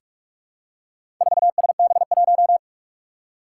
Morse code, heard as a keyed tone, about 46 wpm, 700 Hz. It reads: 4SB1
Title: Four Challenging Letter-Number Combinations 45wpm